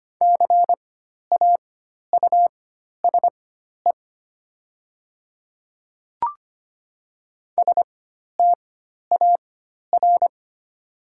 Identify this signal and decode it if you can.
Morse code, heard as a keyed tone, about 25 wpm, spaced as for 12 wpm, 690 Hz. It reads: CAUSE STAR
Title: 4x Character Spacing - Top 500 Words - 25wpm